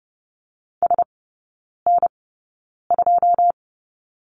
Morse code, heard as a keyed tone, about 30 wpm, spaced as for 9 wpm, 700 Hz. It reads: SN2
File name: Lesson 15e - Number 5 - Three Characters 30wpm